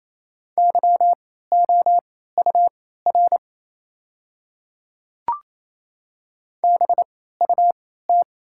Morse code, heard as a keyed tone, about 28 wpm, spaced as for 16 wpm, 700 Hz. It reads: YOUR BUT
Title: 3x Character Spacing - Top 200 Words - 28wpm